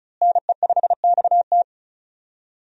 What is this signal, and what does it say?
Morse code, heard as a keyed tone, about 35 wpm, 700 Hz. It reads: NE5XT
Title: USA Call Signs - 5 Characters - Repeated 3x - 35wpm